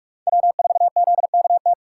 Morse code, heard as a keyed tone, about 43 wpm, 700 Hz. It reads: W4ZKT
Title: USA Call Signs - 3x Character Spacing - 45wpm